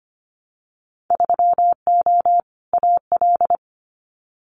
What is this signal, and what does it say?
Morse code, heard as a keyed tone, about 25 wpm, 690 Hz. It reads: 3O AL
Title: Winter Field Day Exchanges Encore 25wpm